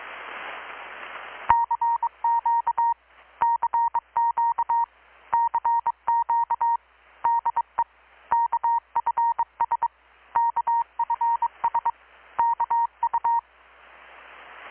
Morse code, heard as a keyed tone, about 22 wpm, 930 Hz. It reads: CQ CQ CQ DE KFS KFS KU
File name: blog / swl / 6-abril-2019-12695-USB-KPS